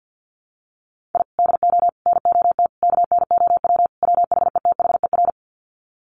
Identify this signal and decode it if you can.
Morse code, heard as a keyed tone, about 50 wpm, 700 Hz. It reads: I DO NOT KNOW WHETHER
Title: Sets of 5 Words 50wpm